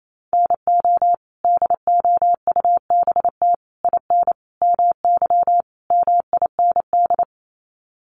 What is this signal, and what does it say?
Morse code, heard as a keyed tone, about 28 wpm, 700 Hz. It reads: NO DOUBT IN MY MIND
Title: Sets of 5 Words Encore2 28wpm